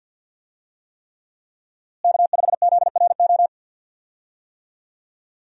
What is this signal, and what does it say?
Morse code, heard as a keyed tone, about 51 wpm, 700 Hz. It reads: K5ZRO
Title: USA Call Signs - 5 Characters - Repeated 3x - 50wpm